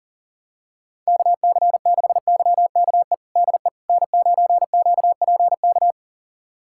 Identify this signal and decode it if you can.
Morse code, heard as a keyed tone, about 40 wpm, 700 Hz. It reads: KC6YKE DE N9QPK
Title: USA Callsign DE Callsign 40wpm